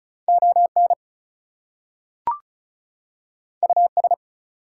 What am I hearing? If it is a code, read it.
Morse code, heard as a keyed tone, about 36 wpm, 700 Hz. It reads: ON US
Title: Two Letter Words - 35wpm